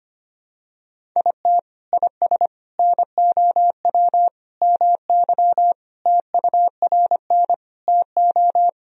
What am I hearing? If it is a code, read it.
Morse code, heard as a keyed tone, about 25 wpm, 700 Hz. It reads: IT IS NOW MY TURN TO
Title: Sentences from Top 200 Words and Abbreviations 25wpm